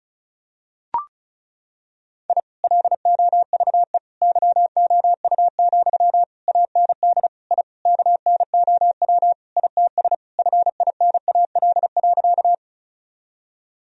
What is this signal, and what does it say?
Morse code, heard as a keyed tone, about 36 wpm, 700 Hz. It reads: I LOVE YOU, AND I KNOW ITS FINAL.
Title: Sentences from Top 1000 Words - Review Part 2 - 35wpm